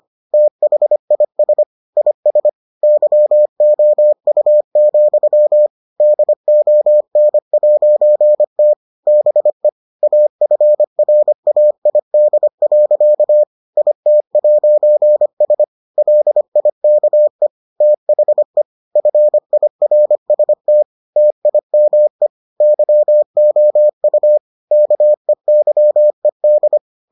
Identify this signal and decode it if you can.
Morse code, heard as a keyed tone, about 25 wpm, 600 Hz. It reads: THIS IS YOU, DON'T BE AFRAID. IT'S LIKE THE FIRST TIME YOU KEYED